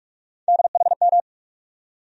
Morse code, heard as a keyed tone, about 46 wpm, 700 Hz. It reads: DHM
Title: USA Call Sign Postfixes - 3 Letters 45wpm